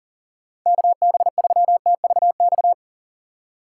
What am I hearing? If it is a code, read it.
Morse code, heard as a keyed tone, about 37 wpm, 700 Hz. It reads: KB3TVX
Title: USA Call Signs - 6x Character Spacing - 40wpm